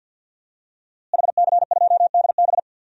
Morse code, heard as a keyed tone, about 50 wpm, 700 Hz. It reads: HC2DB
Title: DXCC-100 Call Signs - 5 Characters - Repeated 2x - 50wpm